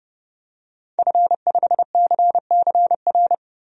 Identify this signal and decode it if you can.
Morse code, heard as a keyed tone, about 32 wpm, 700 Hz. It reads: F5CCR
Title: DXCC-100 Call Signs - 5 Characters - Repeated 3x - 30wpm